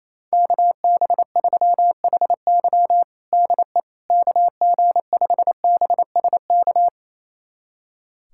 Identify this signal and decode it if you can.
Morse code, heard as a keyed tone, about 28 wpm, 700 Hz. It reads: KB3HY DE KG5BSK
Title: USA Callsign DE Callsign 28wpm